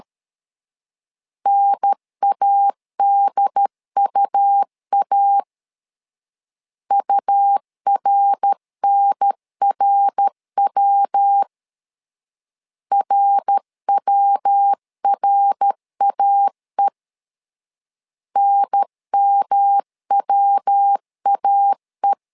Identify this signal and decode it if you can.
Morse code, heard as a keyed tone, about 13 wpm, 800 Hz. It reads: NADUA URNRW RWRAE NMWAE
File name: LEZIONI / 3 MWRO / 3 MWR 12wpm 1711